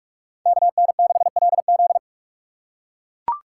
Morse code, heard as a keyed tone, about 42 wpm, 700 Hz. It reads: KN6LZ
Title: USA Call Signs - 5 Characters - 6x Character Spacing - 45wpm